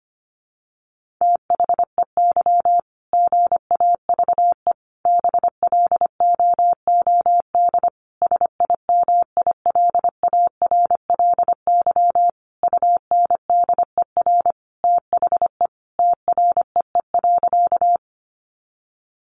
Morse code, heard as a keyed tone, about 25 wpm, 700 Hz. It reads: THEY GAVE BLOOD SIMILARLY UNDER THE TREE.
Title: Sentences from Top 1400 Words 25wpm